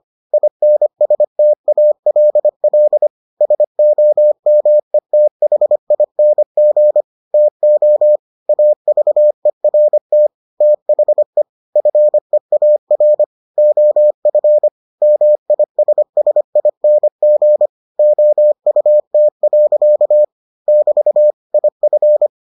Morse code, heard as a keyed tone, about 25 wpm, 600 Hz. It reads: INSTALL SOMETHING TO AVERT THE FEAR OF MISSING OUT. <BT> IF